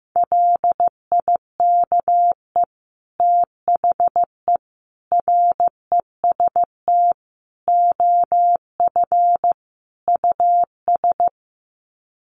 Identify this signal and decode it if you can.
Morse code, heard as a keyed tone, about 15 wpm, 700 Hz. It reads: LIKE THE REST OF US
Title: Sets of 5 Words Encore2 15wpm